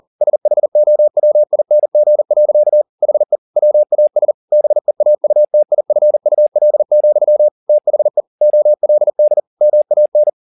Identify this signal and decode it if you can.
Morse code, heard as a keyed tone, about 40 wpm, 600 Hz. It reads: SHOWING. HE WAS BEAUTIFUL, THE OLD MAN